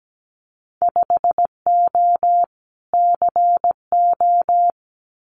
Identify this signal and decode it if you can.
Morse code, heard as a keyed tone, about 17 wpm, 700 Hz. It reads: 5O CO
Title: Winter Field Day Exchanges Encore 17wpm